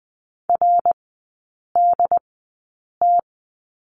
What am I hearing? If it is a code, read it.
Morse code, heard as a keyed tone, about 20 wpm, spaced as for 8 wpm, 700 Hz. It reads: RDT
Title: 067 Lesson 12e - Letter d - Three Characters 20wpm